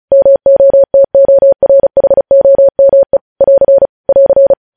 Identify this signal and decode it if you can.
Morse code, heard as a keyed tone, about 36 wpm, 550 Hz. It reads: MOTORHOME <AR> <AR>
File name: podcast / News headlines-1x-35x35